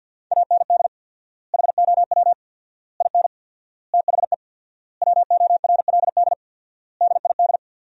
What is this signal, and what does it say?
Morse code, heard as a keyed tone, about 50 wpm, 700 Hz. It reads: AND HOW IN THE WORLD DID
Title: 4x Word-Spacing - Sentences from Top 200 Words - 50wpm